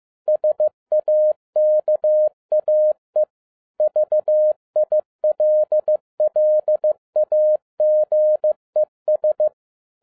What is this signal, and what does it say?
Morse code, heard as a keyed tone, about 15 wpm, 610 Hz. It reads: SAKAE VILLAGES
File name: 2011-10-mp3 / n02-FIELD DAY CONTEST QRV FROM SAKAE VILLAGE 15wpm